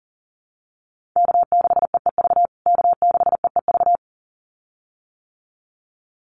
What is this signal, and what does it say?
Morse code, heard as a keyed tone, about 40 wpm, 700 Hz. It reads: K6EEV K6EEV
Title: USA Call Signs Repeated 2x 40wpm